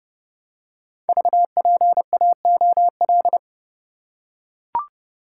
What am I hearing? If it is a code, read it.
Morse code, heard as a keyed tone, about 30 wpm, 700 Hz. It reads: VPAOL
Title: Code Groups - 5 Letters - 30wpm